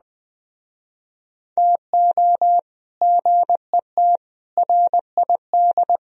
Morse code, heard as a keyed tone, about 24 wpm, 700 Hz. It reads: TO GET RID